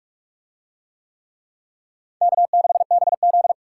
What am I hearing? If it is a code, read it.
Morse code, heard as a keyed tone, about 41 wpm, 700 Hz. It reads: K6BZ